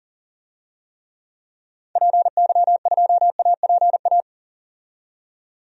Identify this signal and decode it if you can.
Morse code, heard as a keyed tone, about 40 wpm, 700 Hz. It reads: PY2APA
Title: DXCC-100 Call Signs - 6 Characters - 40wpm